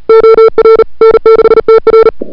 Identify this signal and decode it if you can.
Morse code, heard as a keyed tone, about 37 wpm, 440 Hz. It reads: OR N6TR
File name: CWSprint / February 2007 CW Sprint / 01-2007 CW Sprint Hour1